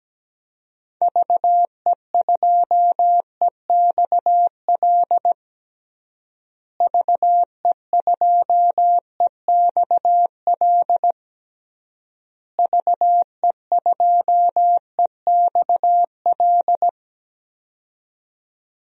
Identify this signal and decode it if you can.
Morse code, heard as a keyed tone, about 17 wpm, 700 Hz. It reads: VE2EXL VE2EXL VE2EXL
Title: DXCC-100 Call Signs - 6 Characters - Repeated 3x - 17wpm